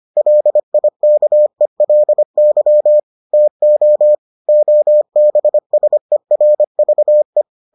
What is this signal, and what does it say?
Morse code, heard as a keyed tone, about 25 wpm, 600 Hz. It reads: LIKELY TO OBSERVE